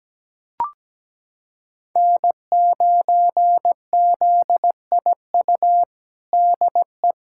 Morse code, heard as a keyed tone, about 17 wpm, 700 Hz. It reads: N9ZIU DE
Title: USA Callsign DE Callsign 17wpm